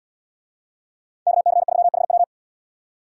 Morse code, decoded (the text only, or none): KC3HW